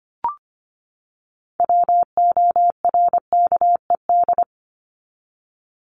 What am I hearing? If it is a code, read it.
Morse code, heard as a keyed tone, about 25 wpm, 700 Hz. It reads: WORKED
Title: ED Suffix - Top 100 Words with Base Words 25wpm